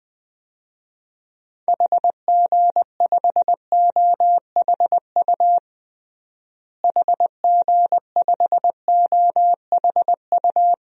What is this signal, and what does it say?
Morse code, heard as a keyed tone, about 20 wpm, 700 Hz. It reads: HG5OHU HG5OHU
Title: DXCC-100 Call Signs - 6 Characters - Repeated 3x - 20wpm